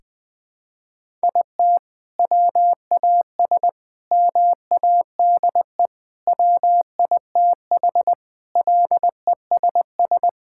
Morse code, heard as a keyed tone, about 19 wpm, 700 Hz. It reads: IT WAS MADE WITH LESS
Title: Sentences from Top 1000 Words - Review Part 5 - 20wpm